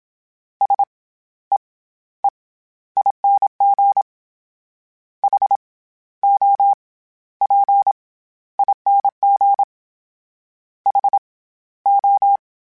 Morse code, heard as a keyed tone, about 27 wpm, 800 Hz. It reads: S E E ING H O P ING H O